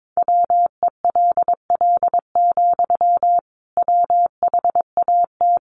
Morse code, heard as a keyed tone, about 22 wpm, 700 Hz. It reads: WELL, WHAT
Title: Sentences from Top 1200 Words 22wpm